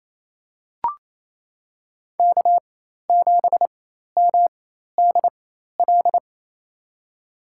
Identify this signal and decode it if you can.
Morse code, heard as a keyed tone, about 28 wpm, spaced as for 13 wpm, 700 Hz. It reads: K7MDL
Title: USA Call Signs - 5 Characters - 4x Character Spacing - 28wpm